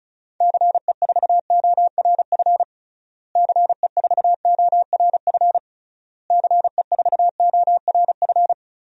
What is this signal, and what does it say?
Morse code, heard as a keyed tone, about 36 wpm, 700 Hz. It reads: CE4ORF CE4ORF CE4ORF
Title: DXCC-100 Call Signs - 6 Characters - Repeated 3x - 35wpm